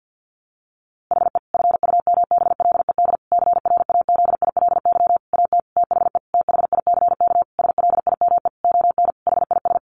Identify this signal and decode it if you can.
Morse code, heard as a keyed tone, about 50 wpm, 700 Hz. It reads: HE FUMBLED CRAZILY AT THE THICK SLIME ON HIS
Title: A Sound Of Thunder 50wpm